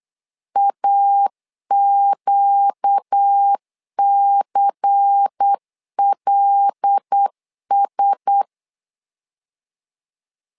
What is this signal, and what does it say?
Morse code, heard as a keyed tone, about 8 wpm, 800 Hz. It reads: AQCLS
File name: LEZIONI / 5 LBVC / 5 LBVC 8wpm 2935